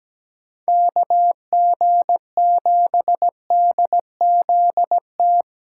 Morse code, heard as a keyed tone, about 17 wpm, 700 Hz. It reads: KG7DZT
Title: USA Callsign DE Callsign - 6 Characters - 17wpm